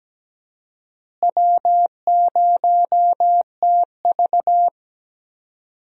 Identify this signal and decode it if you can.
Morse code, heard as a keyed tone, about 17 wpm, 700 Hz. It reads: W0TV
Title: USA Call Signs - 4 Characters - Repeated 3x - 17wpm